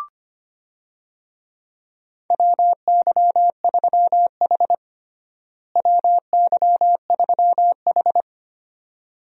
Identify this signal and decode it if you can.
Morse code, heard as a keyed tone, about 25 wpm, 700 Hz. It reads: WY3H WY3H
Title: USA Call Signs - 4 Characters - Repeated 2x - 25wpm